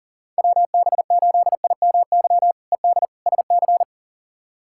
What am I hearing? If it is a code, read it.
Morse code, heard as a keyed tone, about 40 wpm, 700 Hz. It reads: WB8IMY ED SC
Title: K1USN SST QSO Practice - 40wpm